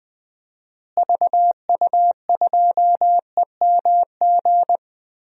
Morse code, heard as a keyed tone, about 20 wpm, 700 Hz. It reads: VU2EMG
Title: DXCC-100 Call Signs - 6 Characters - Repeated 2x - 20wpm